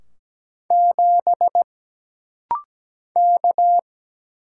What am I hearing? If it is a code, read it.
Morse code, heard as a keyed tone, about 17 wpm, 700 Hz. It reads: 7 K
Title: Single Letter-Number - Rapid-Fire 17wpm